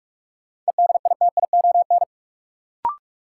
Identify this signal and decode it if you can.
Morse code, heard as a keyed tone, about 46 wpm, 700 Hz. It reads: EDITION
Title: Sets of 5 Words - New Word Drill - 45wpm